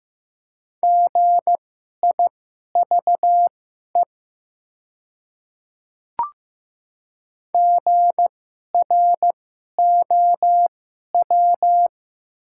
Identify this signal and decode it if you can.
Morse code, heard as a keyed tone, about 15 wpm, spaced as for 11 wpm, 700 Hz. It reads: GIVE GROW
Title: 2x Character Spacing - Top 300 Words - 15wpm